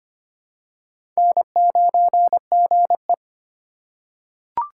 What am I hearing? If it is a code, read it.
Morse code, heard as a keyed tone, about 25 wpm, 700 Hz. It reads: N9GE
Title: USA Call Signs - 4 Characters - 3x Character Spacing - 25wpm